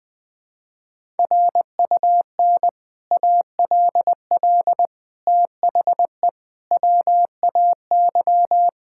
Morse code, heard as a keyed tone, about 20 wpm, 700 Hz. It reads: RUN ALL THE WAY
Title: Sentences with 3-Word Phrases - 20wpm